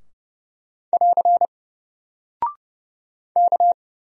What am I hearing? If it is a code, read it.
Morse code, heard as a keyed tone, about 28 wpm, 700 Hz. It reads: <AR> K
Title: Single Character (Letter:Number:Punctuation:Prosign) 30wpm